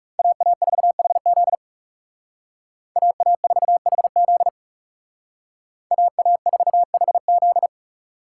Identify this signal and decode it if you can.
Morse code, heard as a keyed tone, about 35 wpm, 700 Hz. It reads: AA4HZ AA4HZ AA4HZ
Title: Speed Racing - USA 2x2 Call Signs 30wpm